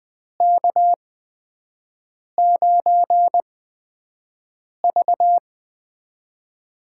Morse code, heard as a keyed tone, about 20 wpm, spaced as for 5 wpm, 700 Hz. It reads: K9V
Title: USA Call Signs - 4 Characters - 8x Character Spacing - 20wpm